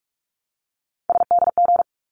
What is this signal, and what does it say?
Morse code, heard as a keyed tone, about 44 wpm, 700 Hz. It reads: SDG
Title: ARRL Sections 45wpm